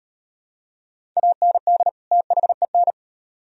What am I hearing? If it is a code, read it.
Morse code, heard as a keyed tone, about 36 wpm, 700 Hz. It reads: AND THEN